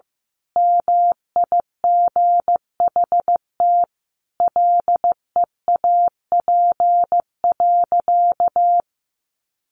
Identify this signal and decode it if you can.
Morse code, heard as a keyed tone, about 15 wpm, 700 Hz. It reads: MIGHT LEAP.
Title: A Sound Of Thunder 15wpm